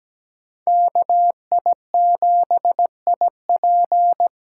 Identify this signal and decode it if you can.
Morse code, heard as a keyed tone, about 17 wpm, 700 Hz. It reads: KI7IP
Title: USA Call Signs - 8x Character Spacing - 17wpm